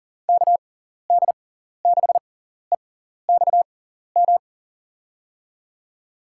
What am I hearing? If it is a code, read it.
Morse code, heard as a keyed tone, about 40 wpm, spaced as for 14 wpm, 700 Hz. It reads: KD6EXM
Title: USA Call Signs - 6 Characters - 6x Character Spacing - 40wpm